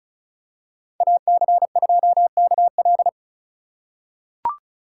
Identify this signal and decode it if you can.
Morse code, heard as a keyed tone, about 34 wpm, 700 Hz. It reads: AC2KL
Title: USA Call Signs - 5 Characters - 3x Character Spacing - 35wpm